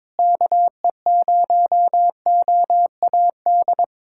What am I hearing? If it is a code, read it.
Morse code, heard as a keyed tone, about 22 wpm, 700 Hz. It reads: KE0OAD
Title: USA Callsign DE Callsign - 6 Characters - 22wpm